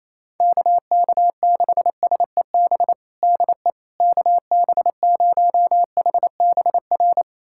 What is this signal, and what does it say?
Morse code, heard as a keyed tone, about 27 wpm, 700 Hz. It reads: KK6SEB DE KB0HBR
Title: USA Callsign DE Callsign - 6 Characters - 28wpm